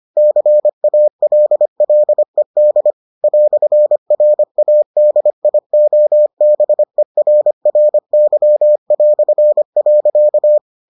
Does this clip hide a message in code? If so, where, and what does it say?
Yes: Morse code from 0.2 to 10.6 s, heard as a keyed tone, about 25 wpm, 600 Hz. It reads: CALLED "RADIOBERRY".